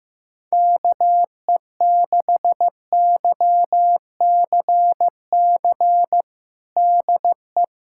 Morse code, heard as a keyed tone, about 16 wpm, 700 Hz. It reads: KE6YCC DE
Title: USA Callsign DE Callsign 15wpm